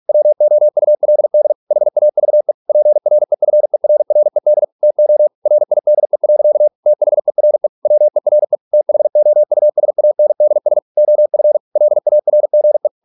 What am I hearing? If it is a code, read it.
Morse code, heard as a keyed tone, about 46 wpm, 600 Hz. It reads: WOULD HAVE PREFERRED TO RIDE. THERE WERE THOUSANDS OF LARGE